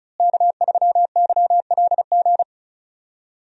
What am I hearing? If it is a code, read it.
Morse code, heard as a keyed tone, about 36 wpm, 700 Hz. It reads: K3YLG
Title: USA Call Signs - 5 Characters - Repeated 2x - 35wpm